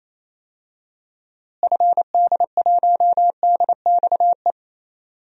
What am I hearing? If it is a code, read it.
Morse code, heard as a keyed tone, about 28 wpm, 700 Hz. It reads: FD1DXE
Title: DXCC-100 Call Signs - 6 Characters - Repeated 3x - 28wpm